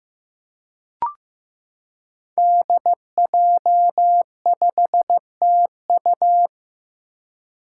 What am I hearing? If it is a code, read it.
Morse code, heard as a keyed tone, about 15 wpm, 700 Hz. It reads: DJ5TU
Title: Five Challenging Letter-Number Combinations 15wpm